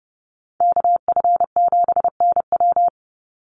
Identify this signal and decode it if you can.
Morse code, heard as a keyed tone, about 30 wpm, 700 Hz. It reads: KF7NW
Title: USA Call Signs 30wpm